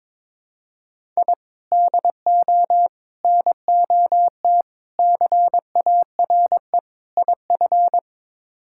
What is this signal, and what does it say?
Morse code, heard as a keyed tone, about 21 wpm, 700 Hz. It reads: I DO NOT CARE IF
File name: practice / 22wpm / Build A Phrase - 5-Word Phrases - 22wpm